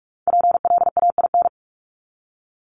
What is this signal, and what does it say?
Morse code, heard as a keyed tone, about 44 wpm, 700 Hz. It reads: PLAIN
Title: Top 500 Words Repeated 45wpm